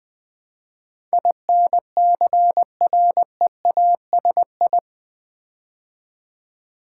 Morse code, heard as a keyed tone, about 20 wpm, 700 Hz. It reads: INCREASI
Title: Build A Word - 4 Letters At A Time - 12-Letter Words - 20wpm